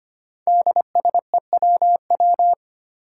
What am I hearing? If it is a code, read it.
Morse code, heard as a keyed tone, about 25 wpm, 700 Hz. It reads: DSEWW
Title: Five Challenging Letter-Number Combinations 25wpm